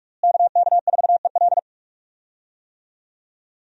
Morse code, heard as a keyed tone, about 44 wpm, 700 Hz. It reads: KK4EL
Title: USA Call Signs - 5 Characters - Repeated 3x - 45wpm